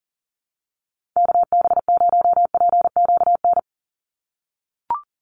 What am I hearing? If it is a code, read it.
Morse code, heard as a keyed tone, about 40 wpm, 690 Hz. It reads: KB0PQN
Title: USA Call Signs 40wpm